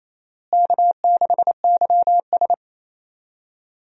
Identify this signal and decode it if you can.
Morse code, heard as a keyed tone, about 27 wpm, 700 Hz. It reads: K6YS
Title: USA Call Signs - 4 Characters - Repeated 2x - 28wpm